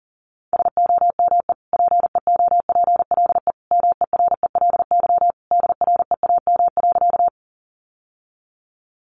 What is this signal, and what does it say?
Morse code, heard as a keyed tone, about 40 wpm, 700 Hz. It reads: SOME PEOPLE MERELY DREAM.
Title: Sentences from Top 1500 Words 40wpm